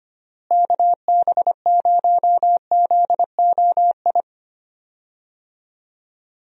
Morse code, heard as a keyed tone, about 25 wpm, 700 Hz. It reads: KB0ZOI
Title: USA Call Signs - 6 Characters - Repeated 2x - 25wpm